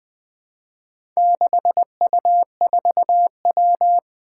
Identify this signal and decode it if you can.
Morse code, heard as a keyed tone, about 21 wpm, 700 Hz. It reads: 6U4W
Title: Four Challenging Letter-Number Combinations 20wpm